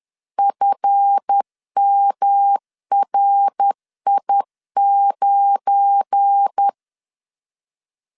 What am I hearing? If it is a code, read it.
Morse code, heard as a keyed tone, about 10 wpm, 800 Hz. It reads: FMRI9
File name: LEZIONI / 7 GF9Z / 7 GF9 10wpm 2747